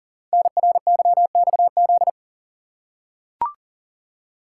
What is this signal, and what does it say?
Morse code, heard as a keyed tone, about 38 wpm, 700 Hz. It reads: NRYXZ
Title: Code Groups - 5 Letters - 40wpm